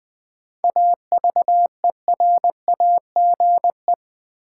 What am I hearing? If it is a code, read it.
Morse code, heard as a keyed tone, about 21 wpm, 700 Hz. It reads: AVERAGE
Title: Sentences from Top 1400 Words - New Word Drill Alt - 20wpm